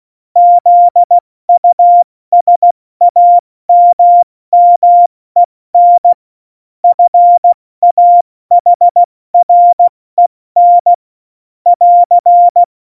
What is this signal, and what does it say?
Morse code, heard as a keyed tone, about 16 wpm, spaced as for 14 wpm, 700 Hz. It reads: ZUSAMMEN FAHREN <AR>
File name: Morsekurs / Witze / Witz 18 Juli 17, Tempo 70 F80